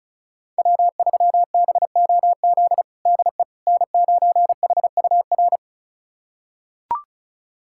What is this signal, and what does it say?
Morse code, heard as a keyed tone, about 36 wpm, 700 Hz. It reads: W3BOZ DE N9HUR
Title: USA Callsign DE Callsign - 5 Characters - 35wpm